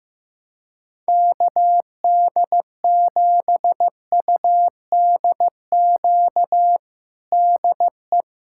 Morse code, heard as a keyed tone, about 15 wpm, 700 Hz. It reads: KD7UDQ DE
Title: USA Callsign DE Callsign - 6 Characters - 15wpm